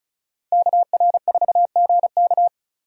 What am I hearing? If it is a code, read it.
Morse code, heard as a keyed tone, about 34 wpm, 700 Hz. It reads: KR4GK
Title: USA Call Signs - 5 Characters - 35wpm